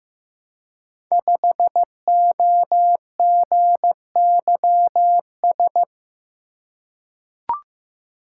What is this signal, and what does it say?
Morse code, heard as a keyed tone, about 15 wpm, 700 Hz. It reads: 5OGYS
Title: Code Groups - 5 Letter:Numbers - 15wpm